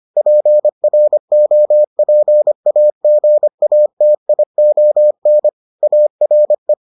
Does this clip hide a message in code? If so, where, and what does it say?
Yes: Morse code from 0.2 to 6.7 s, heard as a keyed tone, about 25 wpm, 600 Hz. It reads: PROPAGATION ARE